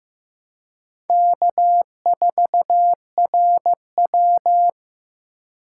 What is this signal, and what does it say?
Morse code, heard as a keyed tone, about 15 wpm, 700 Hz. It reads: K4RW
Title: USA Call Signs - 4 Characters - 2x Character Spacing - 15wpm